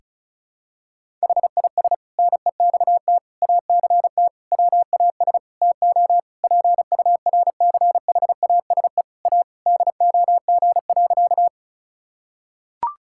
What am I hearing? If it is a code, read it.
Morse code, heard as a keyed tone, about 36 wpm, 700 Hz. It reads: HIS NEXT ACT WAS TO PURCHASE A DOG.
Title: Sentences from Top 1000 Words - Review Part 5 - 35wpm